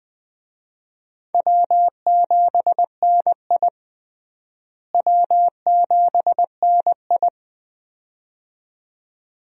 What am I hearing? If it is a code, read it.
Morse code, heard as a keyed tone, about 20 wpm, 700 Hz. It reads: W7NI W7NI
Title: USA Call Signs - 4 Characters - Repeated 2x - 20wpm